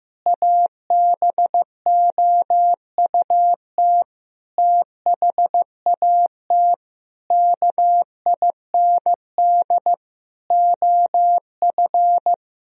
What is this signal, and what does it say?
Morse code, heard as a keyed tone, about 16 wpm, 700 Hz. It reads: ABOUT THAT KIND OF
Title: Sentences from Top 1000 Words - Review Part 4 - 15wpm